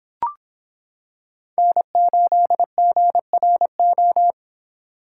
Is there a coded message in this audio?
Yes: Morse code from 1.6 to 4.3 s, heard as a keyed tone, about 26 wpm, 700 Hz. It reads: N8GRO